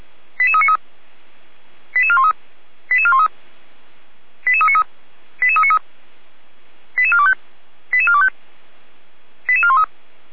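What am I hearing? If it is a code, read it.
Five-tone selective call (ZVEI), heard as quick runs of five tones, tones of about 70 ms. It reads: 88393 88413 88413 88393 88393 88425 88425 88413
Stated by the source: Five-tone ZVEI selective call received on SDR.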